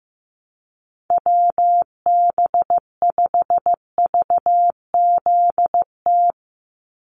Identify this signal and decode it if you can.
Morse code, heard as a keyed tone, about 15 wpm, 700 Hz. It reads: WB5VZT
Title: USA Call Signs 15wpm